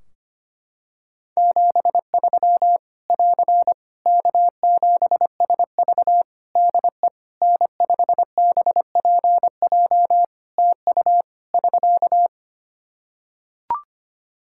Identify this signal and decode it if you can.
Morse code, heard as a keyed tone, about 25 wpm, 700 Hz. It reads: 73 <AR> K7SV DE N5BPJ TU <SK>
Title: Practice QSO - Exchange 4 - Ending - 25wpm